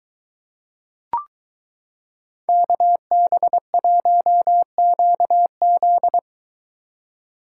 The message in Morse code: KB1QZ

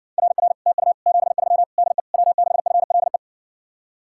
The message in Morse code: RR TU 73 DE W6FLE

